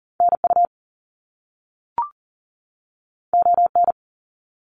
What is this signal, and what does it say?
Morse code, heard as a keyed tone, about 40 wpm, 700 Hz. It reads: NU ON
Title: Canadian Provinces and Territories 40wpm